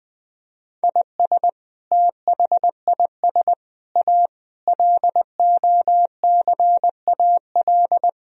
Morse code, heard as a keyed tone, about 20 wpm, 700 Hz. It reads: IS THIS A LOCAL
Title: Sentences from Top 1000 Words - Review Part 3 - 20wpm